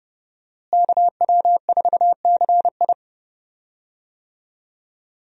Morse code, heard as a keyed tone, about 30 wpm, 700 Hz. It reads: KW4CI
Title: USA Call Signs - 5 Characters - Repeated 2x - 30wpm